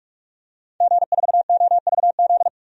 Morse code, heard as a keyed tone, about 41 wpm, 700 Hz. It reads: G4OVZ
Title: DXCC-100 Call Signs - 5 Characters - 45wpm